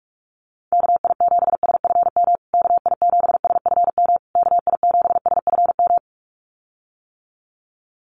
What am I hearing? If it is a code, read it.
Morse code, heard as a keyed tone, about 44 wpm, 700 Hz. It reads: KI7SFM KI7SFM KI7SFM
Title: USA Call Signs Repeated 3x 45wpm